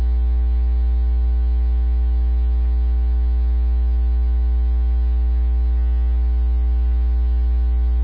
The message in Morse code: none

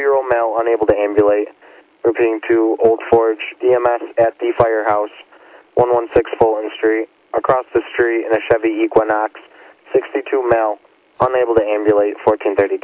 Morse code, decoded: none

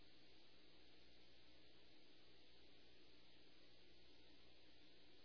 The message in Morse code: none

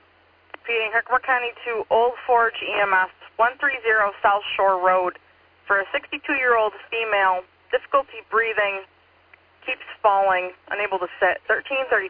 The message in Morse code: none

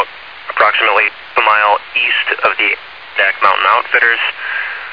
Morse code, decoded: none

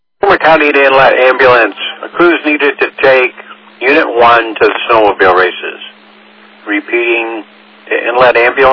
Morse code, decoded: none

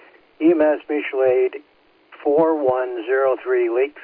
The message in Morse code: none

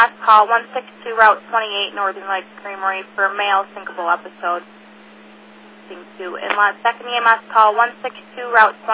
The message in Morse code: none

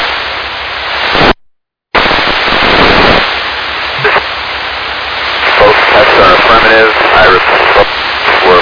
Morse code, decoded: none

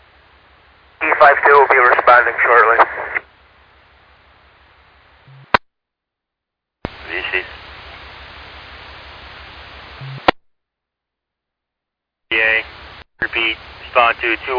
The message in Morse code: none

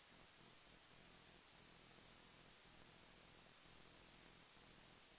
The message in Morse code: none

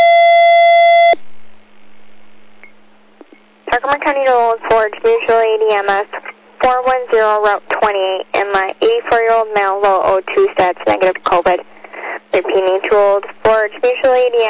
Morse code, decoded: none

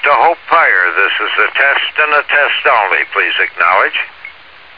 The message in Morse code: none

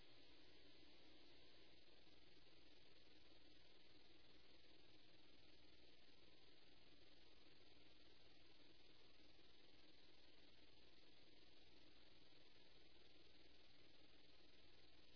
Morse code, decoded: none

